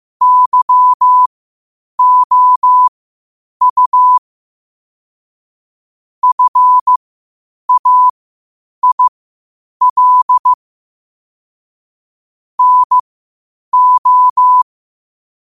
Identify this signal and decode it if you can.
Morse code, heard as a keyed tone, about 15 wpm, spaced as for 8 wpm, 1000 Hz. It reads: YOU FAIL NO
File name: qotdfiles / 8wpm